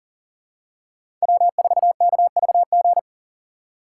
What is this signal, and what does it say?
Morse code, heard as a keyed tone, about 40 wpm, 700 Hz. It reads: W4KVG
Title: USA Call Signs - 5 Characters - 3x Character Spacing - 40wpm